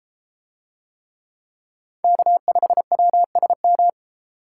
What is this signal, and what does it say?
Morse code, heard as a keyed tone, about 32 wpm, 700 Hz. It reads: K5WSM